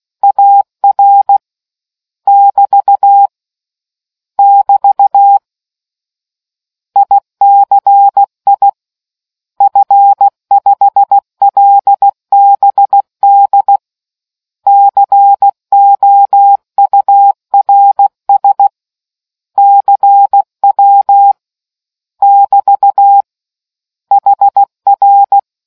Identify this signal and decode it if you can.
Morse code, heard as a keyed tone, about 16 wpm, 800 Hz. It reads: AR <BT> <BT> ICI F5LBD COURS CW <BT> HR